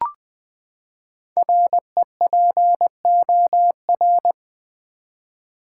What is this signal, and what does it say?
Morse code, heard as a keyed tone, about 20 wpm, 700 Hz. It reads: REPOR
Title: Build A Word - 5 Letters At A Time - 10-Letter Words - 20wpm